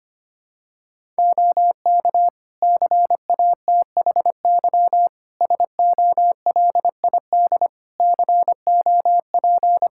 Morse code, heard as a keyed tone, about 25 wpm, 700 Hz. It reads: OK CATHY SOLID COP
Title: K1USN SST QSO Practice 25wpm